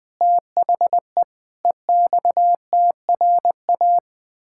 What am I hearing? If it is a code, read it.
Morse code, heard as a keyed tone, about 20 wpm, 700 Hz. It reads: THE EXTRA
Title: Pangram Challenge v2 - 20wpm